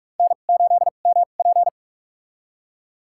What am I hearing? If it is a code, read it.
Morse code, heard as a keyed tone, about 46 wpm, spaced as for 32 wpm, 700 Hz. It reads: N8MP
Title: USA Call Signs - 4 Characters - 2x Character Spacing - 45wpm